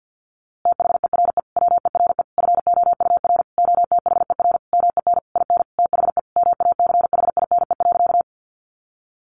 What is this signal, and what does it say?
Morse code, heard as a keyed tone, about 50 wpm, 700 Hz. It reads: THERE WERE FOUR OTHER MEN IN THE MACHINE.
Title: A Sound Of Thunder 50wpm